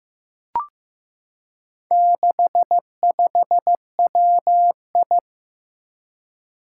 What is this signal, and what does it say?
Morse code, heard as a keyed tone, about 15 wpm, 700 Hz. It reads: 65WI
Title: Four Challenging Letter-Number Combinations 15wpm